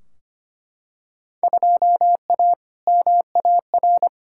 Morse code, heard as a keyed tone, about 25 wpm, 700 Hz. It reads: 2A MAR
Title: Field Day Exchanges - 25wpm